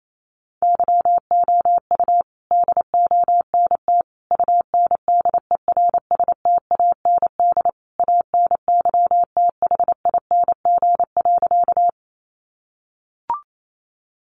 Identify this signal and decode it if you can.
Morse code, heard as a keyed tone, about 28 wpm, 700 Hz. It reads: YOU DONT UNDERSTAND ANYTHING.
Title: Sentences from Top 1000 Words 28wpm